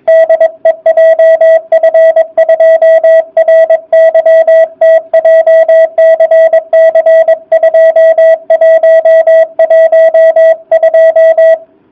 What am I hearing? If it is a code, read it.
Morse code, heard as a keyed tone, about 22 wpm, 660 Hz. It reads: DE JF2RYT JCC 2112